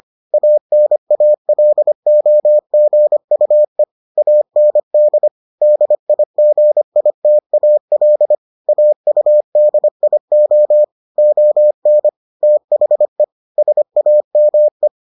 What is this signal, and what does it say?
Morse code, heard as a keyed tone, about 25 wpm, 600 Hz. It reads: ANALOGUE AND DIGITAL AUDIO ON THE SAME